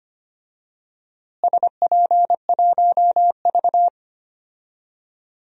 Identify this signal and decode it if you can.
Morse code, heard as a keyed tone, about 25 wpm, 700 Hz. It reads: SP1V
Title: DXCC-100 Call Signs - 4 Characters - Repeated 3x - 25wpm